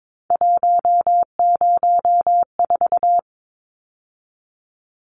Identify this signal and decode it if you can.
Morse code, heard as a keyed tone, about 22 wpm, 700 Hz. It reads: 104
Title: Numbers - 3-Digit 22wpm